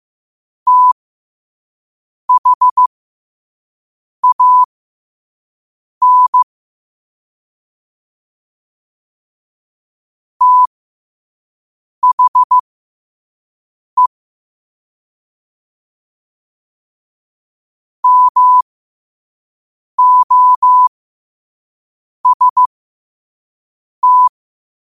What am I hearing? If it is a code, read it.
Morse code, heard as a keyed tone, about 15 wpm, spaced as for 5 wpm, 1000 Hz. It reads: THAN THE MOST
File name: qotdfiles / 5wpm